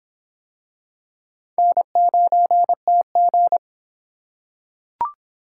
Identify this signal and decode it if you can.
Morse code, heard as a keyed tone, about 25 wpm, 700 Hz. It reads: N9TG